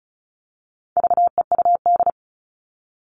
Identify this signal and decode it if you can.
Morse code, heard as a keyed tone, about 34 wpm, 700 Hz. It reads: VEUD
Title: All Cut Numbers - 4-Digit 35wpm